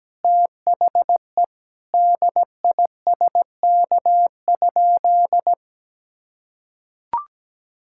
Morse code, heard as a keyed tone, about 17 wpm, 700 Hz. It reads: THE DISK?
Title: Sentences from Top 1000 Words - Review Part 6 - 17wpm